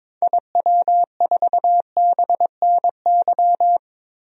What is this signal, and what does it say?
Morse code, heard as a keyed tone, about 21 wpm, 700 Hz. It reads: IW4BNY
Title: DXCC-100 Call Signs - 6 Characters - Repeated 3x - 22wpm